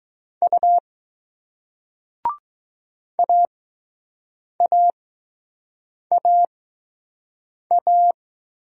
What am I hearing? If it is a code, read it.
Morse code, heard as a keyed tone, about 20 wpm, spaced as for 6 wpm, 700 Hz. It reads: U AAAA